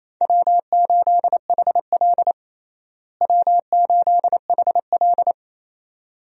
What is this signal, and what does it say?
Morse code, heard as a keyed tone, about 28 wpm, 700 Hz. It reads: W8HL W8HL
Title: USA Call Signs - 4 Characters - Repeated 2x - 28wpm